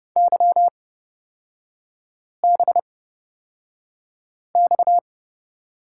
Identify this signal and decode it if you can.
Morse code, heard as a keyed tone, about 31 wpm, spaced as for 5 wpm, 700 Hz. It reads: YBX
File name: Lesson 38e - Letter q - Three Characters 30wpm